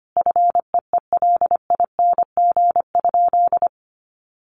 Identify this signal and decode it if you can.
Morse code, heard as a keyed tone, about 25 wpm, 700 Hz. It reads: FEELING?
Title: Sentences from Top 1200 Words 25wpm